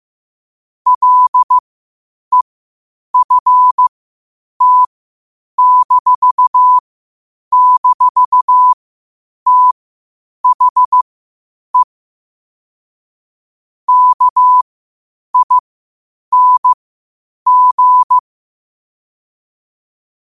Morse code, heard as a keyed tone, about 15 wpm, spaced as for 8 wpm, 1000 Hz. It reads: LEFT--THE KING
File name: qotdfiles / 8wpm